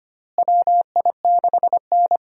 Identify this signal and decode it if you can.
Morse code, heard as a keyed tone, about 25 wpm, 700 Hz. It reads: WI6N
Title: USA Call Signs - 4 Characters - Repeated 3x - 25wpm